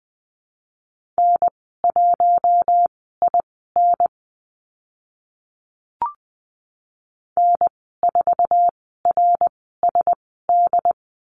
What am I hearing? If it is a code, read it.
Morse code, heard as a keyed tone, about 20 wpm, spaced as for 14 wpm, 700 Hz. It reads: N1IN N4RSD
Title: 069 Lesson 12g - Letter d - Call Signs 20wpm